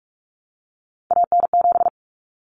Morse code, heard as a keyed tone, about 44 wpm, 700 Hz. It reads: AN7
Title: Common Cut Numbers - 3-Digit 45wpm